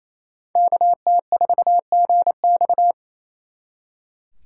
Morse code, heard as a keyed tone, about 36 wpm, 700 Hz. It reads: KT4GX